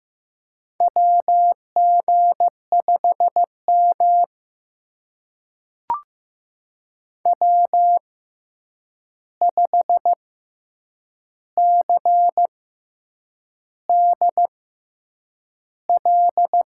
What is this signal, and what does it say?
Morse code, heard as a keyed tone, about 15 wpm, spaced as for 7 wpm, 700 Hz. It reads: WG5M W5CDL
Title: USA Call Signs - 6x Character Spacing - 15wpm